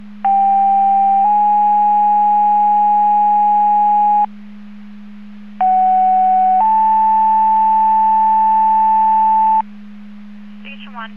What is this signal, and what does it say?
Two-tone page, heard as a one-second tone, then a three-second tone, about 790 then 830 Hz, about 750 then 880 Hz, with speech. It is heard from 0.2 to 9.6 s.